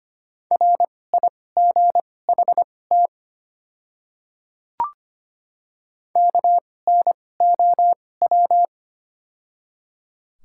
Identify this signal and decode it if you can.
Morse code, heard as a keyed tone, about 25 wpm, spaced as for 18 wpm, 690 Hz. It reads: RIGHT KNOW
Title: 2x Character Spacing - Top 200 Words - 25wpm